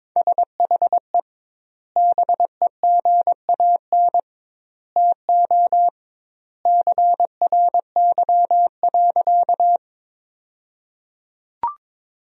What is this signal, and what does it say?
Morse code, heard as a keyed tone, about 22 wpm, 700 Hz. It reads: SHE BEGAN TO CRY.
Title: 2x Word-Spacing - Sentences from Top 500 Words - 22wpm